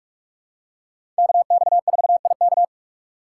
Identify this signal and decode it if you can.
Morse code, heard as a keyed tone, about 46 wpm, 700 Hz. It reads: KX4IK
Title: USA Call Signs - 5 Characters - 4x Character Spacing - 45wpm